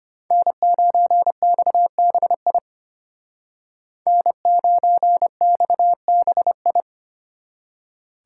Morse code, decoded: N9XBI N9XBI

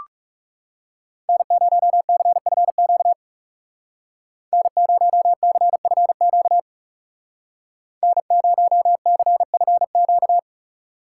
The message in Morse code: N0CFQ N0CFQ N0CFQ

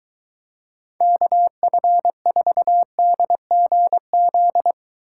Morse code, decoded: KF4DGZ

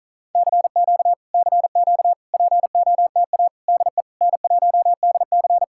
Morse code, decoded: CQ CQ POTA DE N1DC

